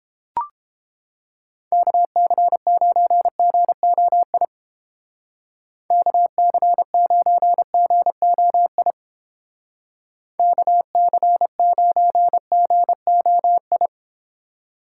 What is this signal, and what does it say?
Morse code, heard as a keyed tone, about 30 wpm, 700 Hz. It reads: KC9GOI KC9GOI KC9GOI